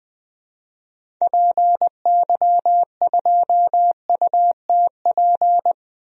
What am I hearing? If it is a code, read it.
Morse code, heard as a keyed tone, about 20 wpm, 700 Hz. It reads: PY2UTP
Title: DXCC-100 Call Signs - 6 Characters - 20wpm